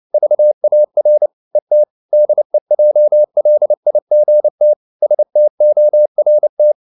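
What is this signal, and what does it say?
Morse code, heard as a keyed tone, about 29 wpm, 600 Hz. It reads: VAR ET DEJLIGT STORT
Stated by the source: Fast (QRQ) Morse at 29 wpm, from CW med Gnister.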